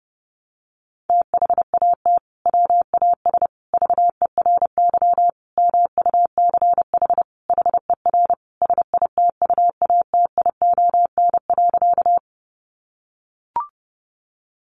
Morse code, heard as a keyed tone, about 30 wpm, 700 Hz. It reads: THAT WAS VERY MUCH HER SITUATION.
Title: Sentences from Top 1200 Words 30wpm